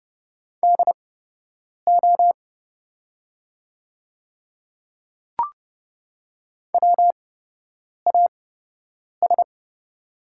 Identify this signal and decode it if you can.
Morse code, heard as a keyed tone, about 31 wpm, spaced as for 8 wpm, 690 Hz. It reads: DO WAS
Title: 8x Character Spacing - Top 200 Words - 30wpm